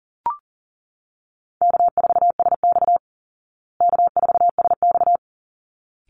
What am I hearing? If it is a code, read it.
Morse code, heard as a keyed tone, about 40 wpm, 700 Hz. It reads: K4SX K4SX
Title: Easy Call Signs Repeated 2x - 40wpm